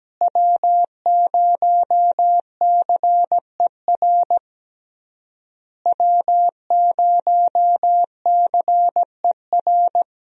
Morse code, heard as a keyed tone, about 17 wpm, 700 Hz. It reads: W0CER W0CER
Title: USA Call Signs - 5 Characters - Repeated 3x - 17wpm